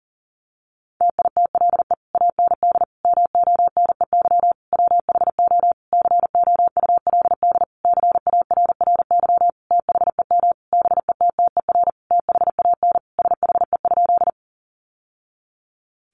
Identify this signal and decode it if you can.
Morse code, heard as a keyed tone, about 40 wpm, 700 Hz. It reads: TITLE AND MONEY WHO COULD CARRY THEM BETTER THAN SHE?
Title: Sentences from Top 1200 Words 40wpm